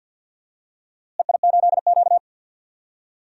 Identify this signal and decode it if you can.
Morse code, heard as a keyed tone, about 47 wpm, 700 Hz. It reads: EI8Q
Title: DXCC-100 Call Signs - 4 Characters - Repeated 2x - 50wpm